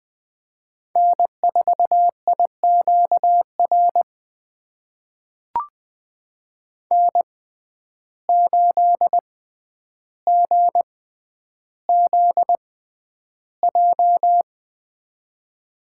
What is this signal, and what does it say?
Morse code, heard as a keyed tone, about 20 wpm, spaced as for 10 wpm, 700 Hz. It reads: N4IQR N8GZJ
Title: USA Call Signs - 5 Characters - 6x Character Spacing - 20wpm